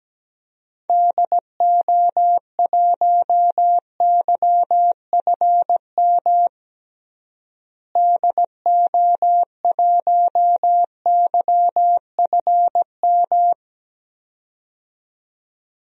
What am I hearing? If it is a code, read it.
Morse code, heard as a keyed tone, about 17 wpm, 700 Hz. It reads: DO1YFM DO1YFM
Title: DXCC-100 Call Signs - 6 Characters - Repeated 2x - 17wpm